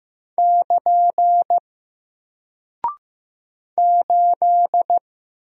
Morse code, heard as a keyed tone, about 15 wpm, 700 Hz. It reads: <KN> 8
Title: Single Character (Letter:Number:Punctuation:Prosign) 15wpm